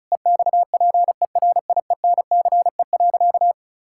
Morse code, heard as a keyed tone, about 36 wpm, 700 Hz. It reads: EXPERIENCE.
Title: Sentences from Top 1000 Words - Review Part 6 - 35wpm